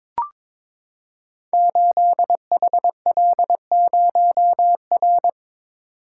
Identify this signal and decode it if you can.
Morse code, heard as a keyed tone, about 23 wpm, 700 Hz. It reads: 8HL0R
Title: Code Groups - 5 Letter:Numbers - 22wpm